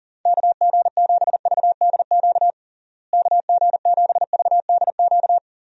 Morse code, heard as a keyed tone, about 40 wpm, 700 Hz. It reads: KG7VDQ KG7VDQ
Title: USA Call Signs - 6 Characters - Repeated 2x - 40wpm